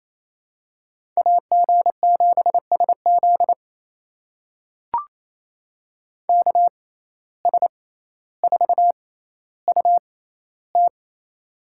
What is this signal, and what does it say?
Morse code, heard as a keyed tone, about 28 wpm, spaced as for 14 wpm, 700 Hz. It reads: AG7SZ KS4UT
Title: USA Call Signs - 5 Characters - 6x Character Spacing - 28wpm